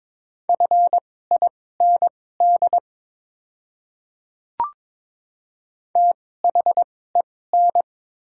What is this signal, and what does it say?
Morse code, heard as a keyed tone, about 22 wpm, spaced as for 16 wpm, 690 Hz. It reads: FIND THEN
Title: 2x Character Spacing - Top 100 Words - 22wpm